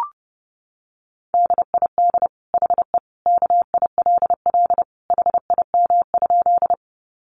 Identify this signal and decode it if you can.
Morse code, heard as a keyed tone, about 30 wpm, 700 Hz. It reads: DID HE KILL HIM?
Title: Sentences from Top 1400 Words 30wpm